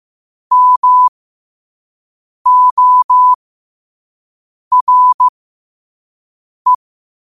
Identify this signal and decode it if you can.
Morse code, heard as a keyed tone, about 15 wpm, spaced as for 5 wpm, 1000 Hz. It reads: MORE
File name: 5wpm